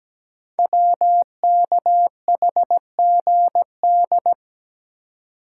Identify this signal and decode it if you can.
Morse code, heard as a keyed tone, about 17 wpm, 700 Hz. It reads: WKHGD
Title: Code Groups - 5 Letters - 17wpm